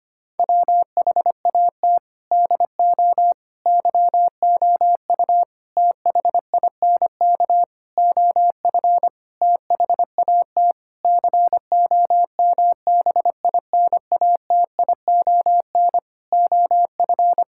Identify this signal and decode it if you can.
Morse code, heard as a keyed tone, about 25 wpm, 700 Hz. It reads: WHAT DO YOU THINK OF THAT COMBINATION OF
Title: Sentences from Top 1000 Words - Review Part 6 - 25wpm